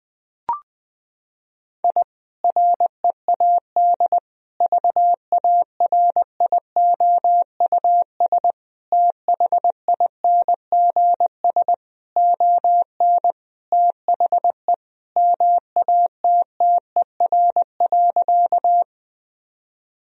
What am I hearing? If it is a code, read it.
Morse code, heard as a keyed tone, about 19 wpm, 700 Hz. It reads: I READ VARIOUS THINGS ON THE MATTER.
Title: Sentences from Top 1000 Words - Review Part 4 - 20wpm